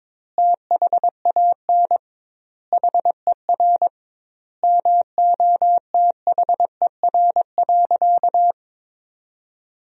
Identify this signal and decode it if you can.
Morse code, heard as a keyed tone, about 22 wpm, 700 Hz. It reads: THAN HER MOTHER.
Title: 2x Word-Spacing - Sentences from Top 400 Words - 22wpm